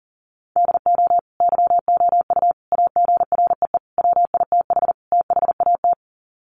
Morse code, heard as a keyed tone, about 40 wpm, 700 Hz. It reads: DO YOU AGREE WITH THAT
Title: Sets of 5 Words Encore 40wpm